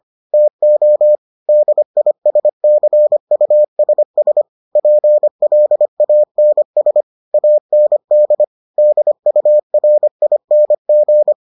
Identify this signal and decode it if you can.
Morse code, heard as a keyed tone, about 25 wpm, 600 Hz. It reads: TO DISCUSS PLANS AND DURING